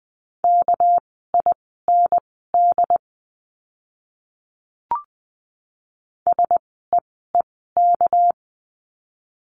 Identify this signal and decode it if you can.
Morse code, heard as a keyed tone, about 20 wpm, spaced as for 14 wpm, 700 Hz. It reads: KIND SEEK
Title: 199 Lesson 32f - Letter k - Words 20wpm